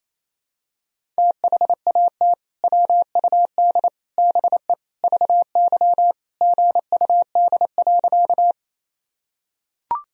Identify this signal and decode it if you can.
Morse code, heard as a keyed tone, about 28 wpm, 700 Hz. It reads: THAT WUD BE VY GUD.
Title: Sentences from Top 200 Words and Abbreviations 28wpm